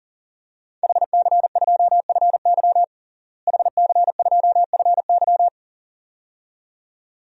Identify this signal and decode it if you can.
Morse code, heard as a keyed tone, about 40 wpm, 700 Hz. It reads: HC2FY HC2FY
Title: DXCC-100 Call Signs - 5 Characters - Repeated 2x - 40wpm